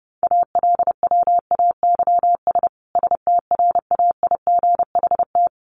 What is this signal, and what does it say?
Morse code, heard as a keyed tone, about 28 wpm, 700 Hz. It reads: ALWAYS STRAIGHT
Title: Sentences from Top 1100 Words 30wpm